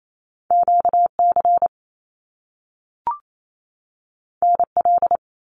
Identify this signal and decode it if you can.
Morse code, heard as a keyed tone, about 28 wpm, 700 Hz. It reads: QC NL
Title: Canadian Provinces and Territories 28wpm